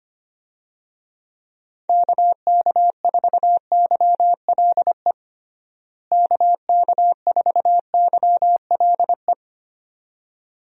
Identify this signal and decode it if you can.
Morse code, heard as a keyed tone, about 25 wpm, 700 Hz. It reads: KK4YLE KK4YLE
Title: USA Call Signs - 6 Characters - Repeated 2x - 25wpm